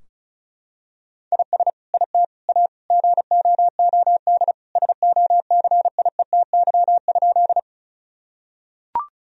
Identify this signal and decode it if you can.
Morse code, heard as a keyed tone, about 36 wpm, 700 Hz. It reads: IS IT A GOOD SOCIETY?
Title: Sentences from Top 1000 Words - Review Part 4 - 35wpm